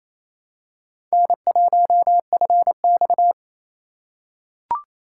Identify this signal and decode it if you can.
Morse code, heard as a keyed tone, about 28 wpm, 700 Hz. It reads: N1FX
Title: USA Call Signs - 4 Characters - 3x Character Spacing - 28wpm